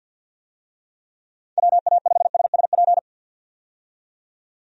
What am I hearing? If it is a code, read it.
Morse code, heard as a keyed tone, about 50 wpm, 700 Hz. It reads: WA5SSP
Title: USA Call Signs - 6 Characters - 50wpm